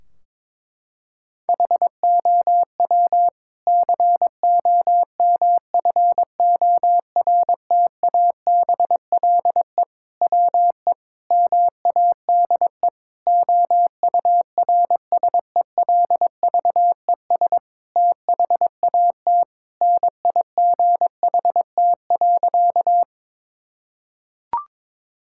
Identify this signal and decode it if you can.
Morse code, heard as a keyed tone, about 22 wpm, 700 Hz. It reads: HOW COMFORTABLE WE MADE OURSELVES THAT NIGHT.
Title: Sentences from Top 1600 Words 22wpm